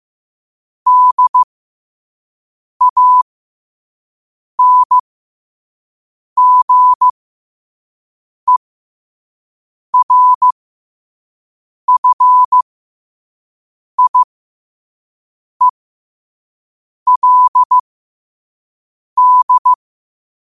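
Morse code, heard as a keyed tone, about 15 wpm, spaced as for 5 wpm, 1000 Hz. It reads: DANGERFIELD
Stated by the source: Morse code sent at 5 wpm.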